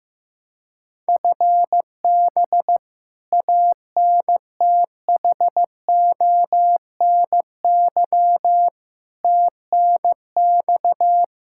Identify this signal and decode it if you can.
Morse code, heard as a keyed tone, about 16 wpm, 700 Hz. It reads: FB ANTHONY TNX
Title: K1USN SST QSO Practice 15wpm